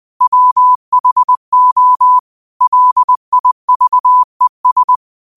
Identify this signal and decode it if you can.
Morse code, heard as a keyed tone, about 20 wpm, 1000 Hz. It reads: WHO LIVES